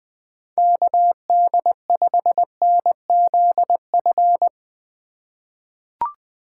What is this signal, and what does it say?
Morse code, heard as a keyed tone, about 20 wpm, 700 Hz. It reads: KD5NZF
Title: USA Call Signs - 6 Characters - 4x Character Spacing - 20wpm